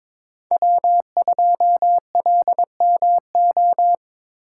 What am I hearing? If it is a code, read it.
Morse code, heard as a keyed tone, about 23 wpm, 700 Hz. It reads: W2LMO
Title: USA Call Signs - 5 Characters - 3x Character Spacing - 22wpm